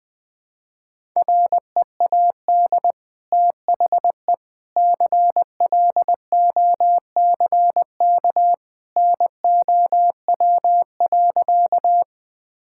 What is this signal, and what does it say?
Morse code, heard as a keyed tone, about 20 wpm, 700 Hz. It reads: READ THE CLOCK NOW.
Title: Easy Sentences with Hints - 20wpm